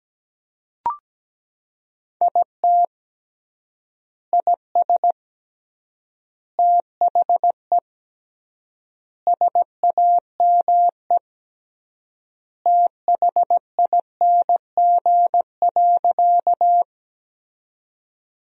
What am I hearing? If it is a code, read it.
Morse code, heard as a keyed tone, about 17 wpm, 700 Hz. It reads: IT IS THE SAME THING.
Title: 3x Word-Spacing - Sentences from Top 200 Words - 17wpm